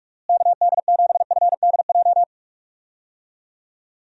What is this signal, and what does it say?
Morse code, heard as a keyed tone, about 46 wpm, 700 Hz. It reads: KD7FDJ
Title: USA Call Signs - 6 Characters - Repeated 3x - 45wpm